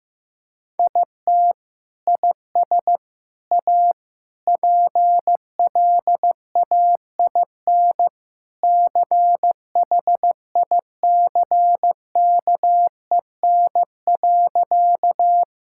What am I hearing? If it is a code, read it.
Morse code, heard as a keyed tone, about 15 wpm, 700 Hz. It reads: IT IS A PLAIN CHICKEN.
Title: Sentences from Top 1000 Words - Review Part 5 - 15wpm